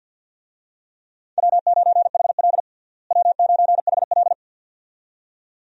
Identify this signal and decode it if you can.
Morse code, heard as a keyed tone, about 51 wpm, 700 Hz. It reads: W9HL W9HL
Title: USA Call Signs - 4 Characters - Repeated 2x - 50wpm